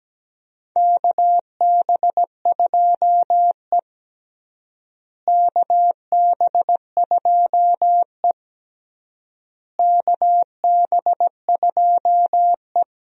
Morse code, heard as a keyed tone, about 17 wpm, 700 Hz. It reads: KB2E KB2E KB2E
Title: USA Call Signs - 4 Characters - Repeated 3x - 17wpm